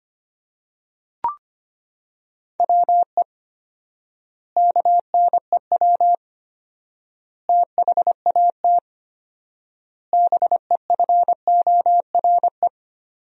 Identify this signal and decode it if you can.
Morse code, heard as a keyed tone, about 25 wpm, 700 Hz. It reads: WE KNEW THAT BEFORE
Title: 4x Word-Spacing - Sentences from Top 400 Words - 25wpm